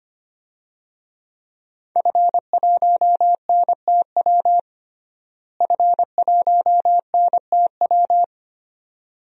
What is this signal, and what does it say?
Morse code, heard as a keyed tone, about 25 wpm, 700 Hz. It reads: F1NTW F1NTW
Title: DXCC-100 Call Signs - 5 Characters - Repeated 2x - 25wpm